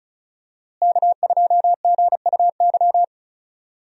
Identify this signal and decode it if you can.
Morse code, heard as a keyed tone, about 36 wpm, 700 Hz. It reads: K2GUY
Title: USA Call Signs - 5 Characters - Repeated 3x - 35wpm